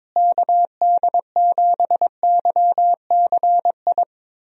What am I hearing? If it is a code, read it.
Morse code, heard as a keyed tone, about 22 wpm, 700 Hz. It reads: KD7YCI
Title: USA Call Signs - 6 Characters - Repeated 2x - 22wpm